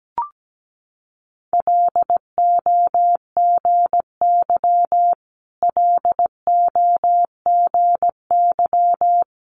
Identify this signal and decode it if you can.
Morse code, heard as a keyed tone, about 17 wpm, 700 Hz. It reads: LOGY LOGY
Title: Multi-letter Phonemes Repeated 3x 17wpm